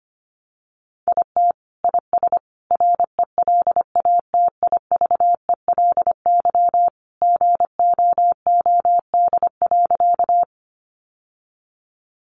Morse code, heard as a keyed tone, about 25 wpm, 700 Hz. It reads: IT IS RELATIVELY GOOD.
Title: Sentences from Top 1100 Words 25wpm